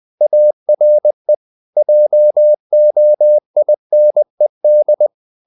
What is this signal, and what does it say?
Morse code, heard as a keyed tone, about 20 wpm, 600 Hz. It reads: ARE JOINED
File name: the old man and the sea / the old man and the sea part 1 20wpm